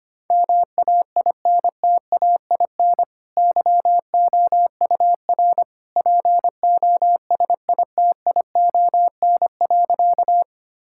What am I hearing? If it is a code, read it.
Morse code, heard as a keyed tone, about 25 wpm, 700 Hz. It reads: MAINTAIN YOUR POSITION.
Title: Sentences from Top 1000 Words - Review Part 5 - 25wpm